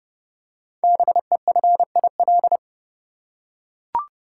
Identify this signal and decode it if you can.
Morse code, heard as a keyed tone, about 30 wpm, 700 Hz. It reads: BEFIL
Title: Code Groups - 5 Letters - 30wpm